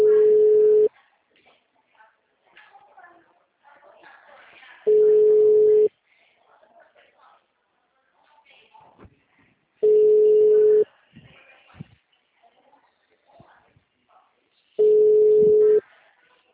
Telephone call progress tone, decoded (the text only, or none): ringback tone